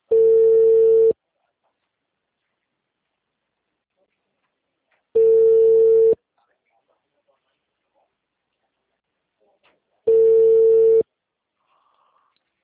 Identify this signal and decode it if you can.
Ringback tone (China, 450 Hz), with speech.